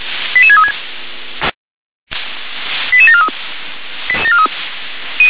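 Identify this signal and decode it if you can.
Five-tone selective call (ZVEI), heard as quick runs of five tones, tones of about 70 ms. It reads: 88636 88743
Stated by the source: ZVEI five-tone selective call 88633 received via SDR.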